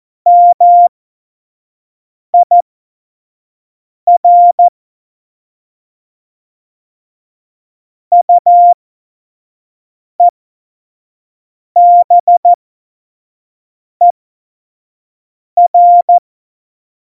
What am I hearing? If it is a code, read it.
Morse code, heard as a keyed tone, about 14 wpm, spaced as for 5 wpm, 700 Hz. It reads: MIR UEBER
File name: Morsekurs / Witze / Witz 02. April 2019, 25 F70